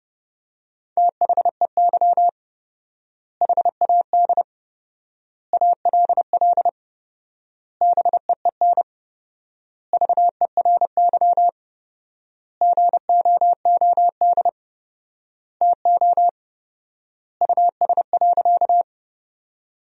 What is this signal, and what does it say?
Morse code, heard as a keyed tone, about 30 wpm, 700 Hz. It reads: THEY HAD ALL BEEN VERY GOOD TO US.
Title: 4x Word-Spacing - Sentences from Top 200 Words - 30wpm